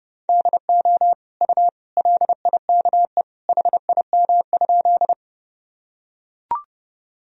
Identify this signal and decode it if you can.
Morse code, heard as a keyed tone, about 31 wpm, 700 Hz. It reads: DO U LIKE HIM?
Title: Sentences from Top 100 Words and Abbreviations 30wpm